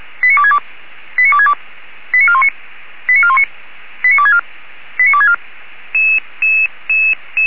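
Five-tone selective call (ZVEI), heard as quick runs of five tones, tones of about 70 ms. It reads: 89262 89262 89419 89419 89264 89264
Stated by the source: ZVEI five-tone call received on SDR.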